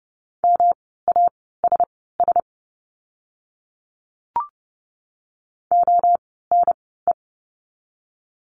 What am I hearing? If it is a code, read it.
Morse code, heard as a keyed tone, about 30 wpm, spaced as for 17 wpm, 700 Hz. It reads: MASS ONE
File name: course / 30:10wpm / 108 Lesson 18f - Letter m - Words 30wpm